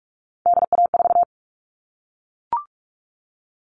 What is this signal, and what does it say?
Morse code, heard as a keyed tone, about 44 wpm, 700 Hz. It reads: DA4
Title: All Cut Numbers - 3-Digit 45wpm